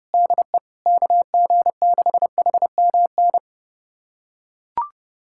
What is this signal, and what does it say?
Morse code, heard as a keyed tone, about 32 wpm, 700 Hz. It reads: DE KG6HMN
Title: USA Callsign DE Callsign 30wpm